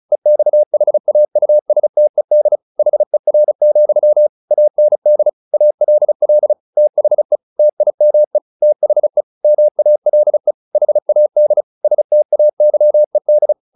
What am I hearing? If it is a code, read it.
Morse code, heard as a keyed tone, about 34 wpm, 600 Hz. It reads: EXHAUSTED HER, AND ALL THE TIME THE MALE HAD STAYED
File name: the old man and the sea / the old man and the sea 1 35wpm